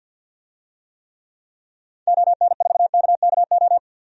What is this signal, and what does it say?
Morse code, heard as a keyed tone, about 48 wpm, 700 Hz. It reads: ON4KKO
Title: DXCC-100 Call Signs - 6 Characters - 50wpm